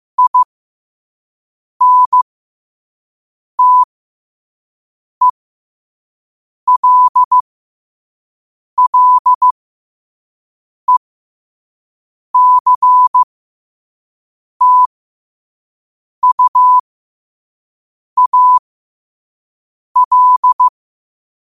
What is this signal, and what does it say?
Morse code, heard as a keyed tone, about 15 wpm, spaced as for 5 wpm, 1000 Hz. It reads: INTELLECTUAL
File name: qotdfiles / 5wpm